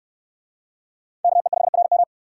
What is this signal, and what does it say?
Morse code, heard as a keyed tone, about 62 wpm, 700 Hz. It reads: KE5RM